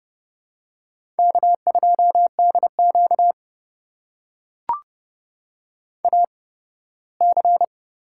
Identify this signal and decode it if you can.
Morse code, heard as a keyed tone, about 30 wpm, spaced as for 18 wpm, 700 Hz. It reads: K2DQ AC
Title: USA Call Signs - 4 Characters - 8x Character Spacing - 30wpm